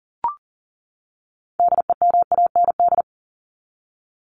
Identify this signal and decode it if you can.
Morse code, heard as a keyed tone, about 40 wpm, 700 Hz. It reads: DEMAND
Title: Top 501-750 Words 40wpm